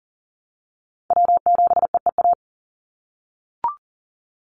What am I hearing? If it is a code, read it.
Morse code, heard as a keyed tone, about 42 wpm, 700 Hz. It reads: W7EEA
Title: USA Call Signs 40wpm